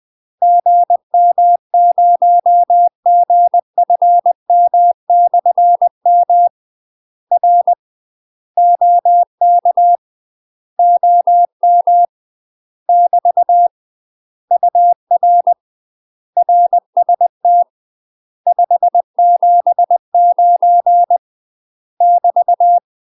Morse code, heard as a keyed tone, about 20 wpm, 700 Hz. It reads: GM0GFM/M R OK OM <BT> UR RST 579 <BT>
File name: UK MASTER QSOs TRAINING / 20WPM / UK MASTER QSOs 0070